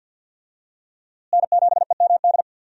Morse code, heard as a keyed tone, about 46 wpm, 700 Hz. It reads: N7EMD
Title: USA Call Signs - 5 Characters - Repeated 3x - 50wpm